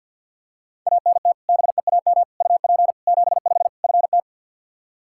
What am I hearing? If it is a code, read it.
Morse code, heard as a keyed tone, about 47 wpm, 700 Hz. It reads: ANT BEAM UP 75 FT
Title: Practice QSO - Exchange 2 - Info - 50wpm